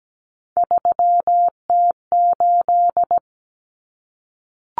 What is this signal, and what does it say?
Morse code, heard as a keyed tone, about 17 wpm, 700 Hz. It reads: 3T8
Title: Common Cut Numbers - 3-Digit 17wpm